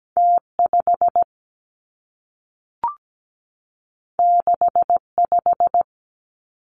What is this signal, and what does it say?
Morse code, heard as a keyed tone, about 17 wpm, 700 Hz. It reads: T5 65
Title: Two Challenging Letter-Number Combinations 17wpm